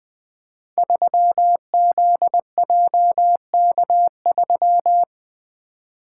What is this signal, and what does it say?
Morse code, heard as a keyed tone, about 20 wpm, 700 Hz. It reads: 3ZJK3
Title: Code Groups - 5 Letter:Numbers - 20wpm